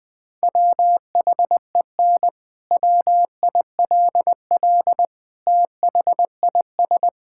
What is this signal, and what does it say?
Morse code, heard as a keyed tone, about 20 wpm, 700 Hz. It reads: WHEN WILL THIS
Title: Sentences from Top 1600 Words 20wpm